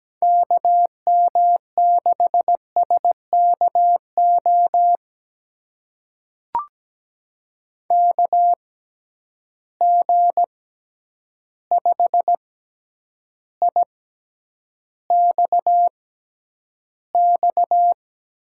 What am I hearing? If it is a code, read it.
Morse code, heard as a keyed tone, about 17 wpm, spaced as for 9 wpm, 700 Hz. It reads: KM6SKO KG5IXX
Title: USA Call Signs - 6 Characters - 6x Character Spacing - 17wpm